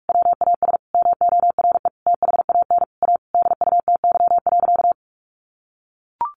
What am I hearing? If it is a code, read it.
Morse code, heard as a keyed tone, about 44 wpm, 700 Hz. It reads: WAS MORE THAN A DUTY.
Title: Sentences from Top 1500 Words 45wpm